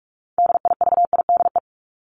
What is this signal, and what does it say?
Morse code, heard as a keyed tone, about 44 wpm, 700 Hz. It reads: DIVIDE
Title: Top 1501-1750 Words 45wpm